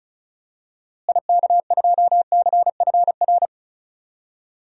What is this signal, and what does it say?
Morse code, heard as a keyed tone, about 34 wpm, 700 Hz. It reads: IK2CFR
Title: DXCC-100 Call Signs - 6 Characters - Repeated 2x - 35wpm